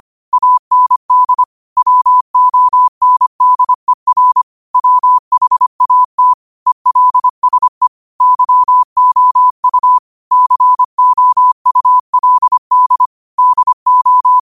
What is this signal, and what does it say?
Morse code, heard as a keyed tone, about 25 wpm, 1000 Hz. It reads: AND WONDER WHAT ELSE YOU COULD DO
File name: qotdfiles / 25wpm